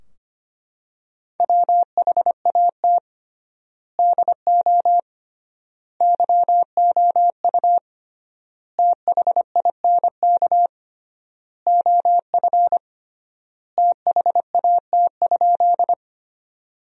Morse code, heard as a keyed tone, about 25 wpm, 700 Hz. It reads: WHAT DO YOU THINK OF THAT?
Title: 3x Word-Spacing - Sentences from Top 200 Words - 25wpm